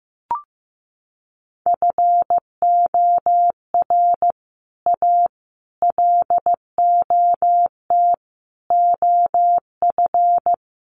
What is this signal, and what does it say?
Morse code, heard as a keyed tone, about 14 wpm, 700 Hz. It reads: FOR A LOT OF
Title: Sets of 5 Words Encore 15wpm